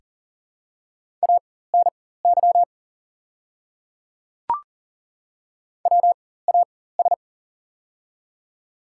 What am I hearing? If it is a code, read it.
Morse code, heard as a keyed tone, about 40 wpm, spaced as for 19 wpm, 700 Hz. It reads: ANY WAS
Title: 4x Character Spacing - Top 200 Words - 40wpm